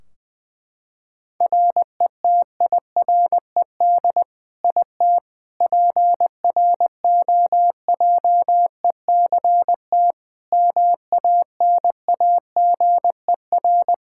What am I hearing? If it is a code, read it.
Morse code, heard as a keyed tone, about 20 wpm, 700 Hz. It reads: RETIRED IT PROJECT MANAGER
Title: Practice QSO - Exchange 3 - More Info - 20wpm